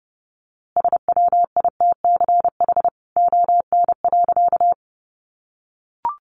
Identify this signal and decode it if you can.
Morse code, heard as a keyed tone, about 28 wpm, 700 Hz. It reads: SWITCH ON.
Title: A Sound Of Thunder 30wpm